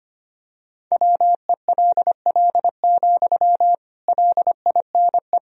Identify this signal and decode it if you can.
Morse code, heard as a keyed tone, about 25 wpm, 700 Hz. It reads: WELL, LINE
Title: Sentences from Top 1000 Words - Review Part 1 - 25wpm